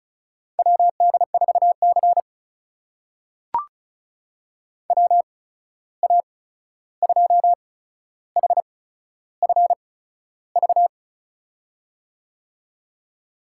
Morse code, heard as a keyed tone, about 36 wpm, spaced as for 13 wpm, 700 Hz. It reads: WD4C WA2HFV
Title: USA Call Signs - 8x Character Spacing - 35wpm